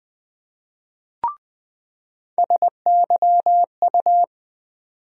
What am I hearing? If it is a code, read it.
Morse code, heard as a keyed tone, about 20 wpm, 700 Hz. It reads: SYU
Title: USA Call Sign Postfixes - 3 Letters 20wpm